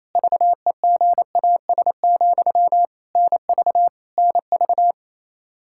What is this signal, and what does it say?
Morse code, heard as a keyed tone, about 28 wpm, 700 Hz. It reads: VEGAS, NV NV
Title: K1USN SST QSO Practice 28wpm